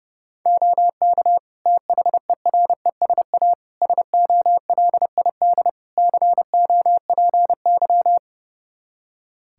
Morse code, heard as a keyed tone, about 31 wpm, 700 Hz. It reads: OK THERESA SOLID COPY
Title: Practice QSO - Exchange 3 - More Info - 30wpm